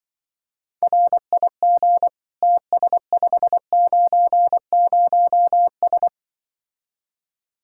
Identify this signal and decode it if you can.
Morse code, heard as a keyed tone, about 24 wpm, 700 Hz. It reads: RIG TS590S
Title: Speed-Racing Common QSO Phrases Encore - 20wpm